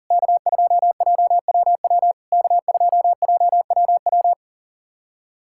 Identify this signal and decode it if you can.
Morse code, heard as a keyed tone, about 40 wpm, 690 Hz. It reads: K2JWW K2JWW
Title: SOTA QSO Practice 40wpm